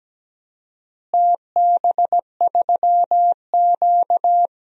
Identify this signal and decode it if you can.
Morse code, heard as a keyed tone, about 17 wpm, 700 Hz. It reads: TB3Q
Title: DXCC-100 Call Signs - 4 Characters - Repeated 2x - 17wpm